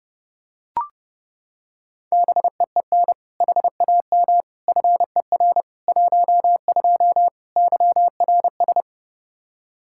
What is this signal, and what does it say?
Morse code, heard as a keyed tone, about 29 wpm, 700 Hz. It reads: BEEN HAM FER 12 YRS
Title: Practice QSO - Exchange 3 - More Info - 30wpm